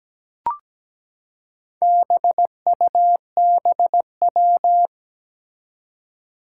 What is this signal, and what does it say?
Morse code, heard as a keyed tone, about 17 wpm, 700 Hz. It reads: BUBW
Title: Four Challenging Letter-Number Combinations 17wpm